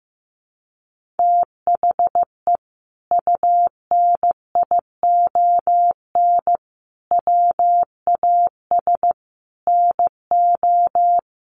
Morse code, heard as a keyed tone, about 15 wpm, 700 Hz. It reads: THE UNION WAS NO
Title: Sentences from Top 1400 Words 15wpm